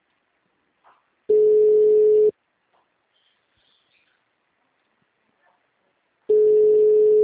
Ringback tone (Europe, 425 Hz).